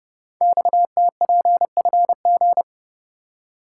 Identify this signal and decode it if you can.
Morse code, heard as a keyed tone, about 30 wpm, 700 Hz. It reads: XTPFG
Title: Code Groups - 5 Letters - 30wpm